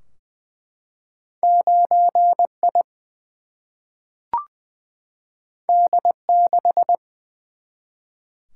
Morse code, heard as a keyed tone, about 20 wpm, 700 Hz. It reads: 9 I D 6
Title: Single-Letter Single-Number 20wpm